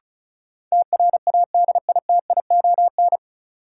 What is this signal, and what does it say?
Morse code, heard as a keyed tone, about 34 wpm, 700 Hz. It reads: TRADITION
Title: Sentences from Top 1500 Words - New Word Drill Alt - 35wpm